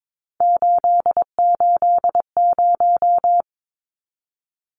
Morse code, heard as a keyed tone, about 22 wpm, 700 Hz. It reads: 880
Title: Numbers - 3-Digit 22wpm